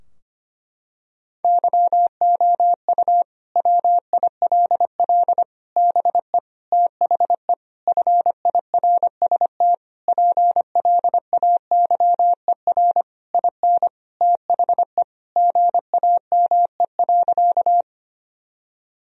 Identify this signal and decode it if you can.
Morse code, heard as a keyed tone, about 25 wpm, 700 Hz. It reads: YOU WILL BE THE FIRST PLAYER IN THE GAME.
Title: Sentences from Top 1000 Words - Review Part 4 - 25wpm